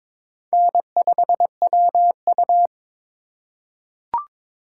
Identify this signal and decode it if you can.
Morse code, heard as a keyed tone, about 22 wpm, 700 Hz. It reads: N5WU
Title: USA Call Signs - 4 Characters - 4x Character Spacing - 22wpm